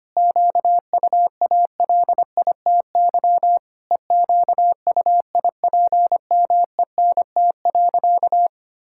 Morse code, heard as a keyed tone, about 25 wpm, 700 Hz. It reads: QUALITY EQUIPMENT.
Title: Sentences from Top 1000 Words - Review Part 6 - 25wpm